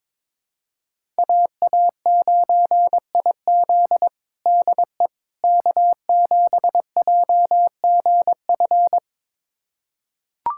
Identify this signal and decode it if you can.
Morse code, heard as a keyed tone, about 22 wpm, 700 Hz. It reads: AA9IZ DE K7JGF
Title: USA Callsign DE Callsign - 5 Characters - 22wpm